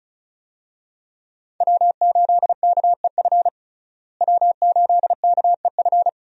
Morse code, heard as a keyed tone, about 36 wpm, 700 Hz. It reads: W8KEF W8KEF
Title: USA Call Signs - 5 Characters - Repeated 2x - 35wpm